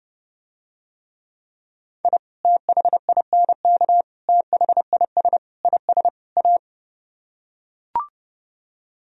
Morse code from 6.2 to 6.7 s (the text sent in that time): A